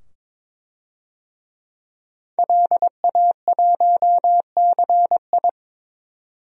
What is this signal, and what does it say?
Morse code, heard as a keyed tone, about 22 wpm, 700 Hz. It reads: LA1CI
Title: DXCC-100 Call Signs - 5 Characters - 22wpm